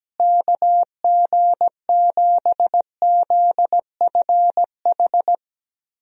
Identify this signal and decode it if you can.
Morse code, heard as a keyed tone, about 17 wpm, 700 Hz. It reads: KG7ZFH
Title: USA Call Signs - 6 Characters - Repeated 2x - 17wpm